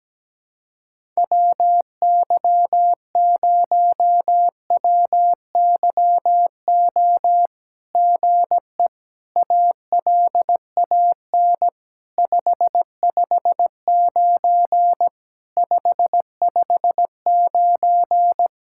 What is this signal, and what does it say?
Morse code, heard as a keyed tone, about 17 wpm, 700 Hz. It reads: WY0WYO GE ALAN 559 559
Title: SOTA QSO Practice 17wpm